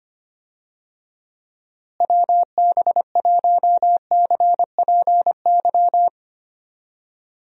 Morse code, heard as a keyed tone, about 25 wpm, 700 Hz. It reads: WB1CPY
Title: USA Call Signs - 6 Characters - Repeated 2x - 25wpm